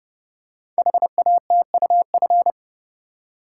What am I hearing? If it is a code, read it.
Morse code, heard as a keyed tone, about 30 wpm, 700 Hz. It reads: HATUF
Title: Code Groups - 5 Letters - 30wpm